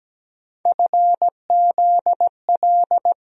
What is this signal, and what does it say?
Morse code, heard as a keyed tone, about 17 wpm, 700 Hz. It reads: FZL
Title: USA Call Sign Postfixes - 3 Letters 17wpm